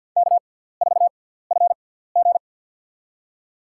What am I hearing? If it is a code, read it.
Morse code, heard as a keyed tone, about 51 wpm, spaced as for 17 wpm, 700 Hz. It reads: K4FG
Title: USA Call Signs - 4 Characters - 6x Character Spacing - 50wpm